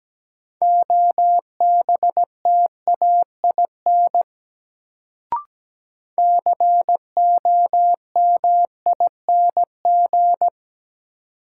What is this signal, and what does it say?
Morse code, heard as a keyed tone, about 17 wpm, 700 Hz. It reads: OBTAIN COMING
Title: Sentences from Top 1300 Words - New Word Drill Alt - 17wpm